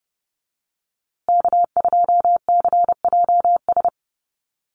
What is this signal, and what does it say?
Morse code, heard as a keyed tone, about 30 wpm, 700 Hz. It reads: K2CJS
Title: USA Call Signs Repeated 3x 30wpm